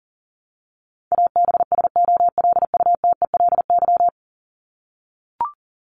Morse code, heard as a keyed tone, about 40 wpm, 700 Hz. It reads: ABSOLUTELY
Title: LY Suffix - Top 100 Words with Base Words 40wpm